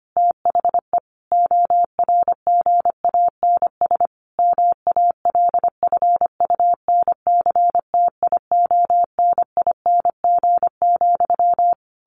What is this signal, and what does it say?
Morse code, heard as a keyed tone, about 25 wpm, 700 Hz. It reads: THE ORGANS MALFUNCTIONING,
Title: A Sound Of Thunder 25wpm